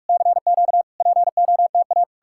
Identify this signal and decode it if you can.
Morse code, heard as a keyed tone, about 59 wpm, 700 Hz. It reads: CQ POTA